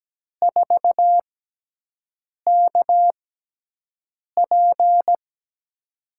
Morse code, heard as a keyed tone, about 17 wpm, spaced as for 6 wpm, 700 Hz. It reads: 4KP
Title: USA Call Signs - 4 Characters - 6x Character Spacing - 17wpm